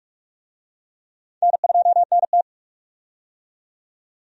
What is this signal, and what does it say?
Morse code, heard as a keyed tone, about 46 wpm, 700 Hz. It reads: N2NT
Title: USA Call Signs - 4 Characters - 45wpm